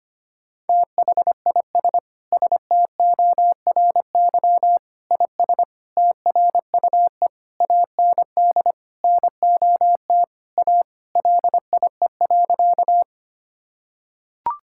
Morse code, heard as a keyed tone, about 25 wpm, 700 Hz. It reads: THIS STORY IS TRUE AND NOT A LIE.
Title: Sentences from Top 1600 Words 25wpm